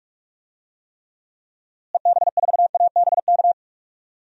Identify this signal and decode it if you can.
Morse code, heard as a keyed tone, about 46 wpm, 700 Hz. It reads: EB4ABK
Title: DXCC-100 Call Signs - 6 Characters - Repeated 3x - 45wpm